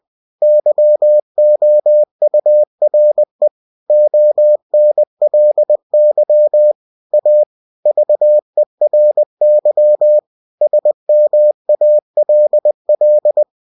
Morse code, heard as a keyed tone, about 20 wpm, 600 Hz. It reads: YOURE ONLY A VERY SMALL
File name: cw-books / winnie-the-pooh / winnie-20wpm-page-0062